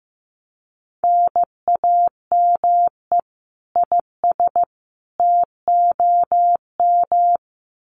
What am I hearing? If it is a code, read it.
Morse code, heard as a keyed tone, about 15 wpm, 700 Hz. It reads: NAME IS TOM
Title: Common QSO Phrases 15wpm